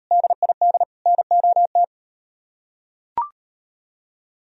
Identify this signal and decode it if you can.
Morse code, heard as a keyed tone, about 38 wpm, 700 Hz. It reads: DID NOT